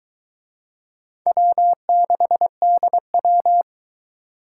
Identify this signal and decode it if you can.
Morse code, heard as a keyed tone, about 23 wpm, 700 Hz. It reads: W6DW